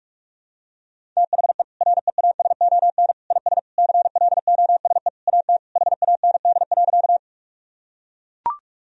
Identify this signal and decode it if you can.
Morse code, heard as a keyed tone, about 42 wpm, 700 Hz. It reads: THE REASON IS CLOSE AT HAND.
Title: Sentences from Top 1000 Words - Review Part 1 - 45wpm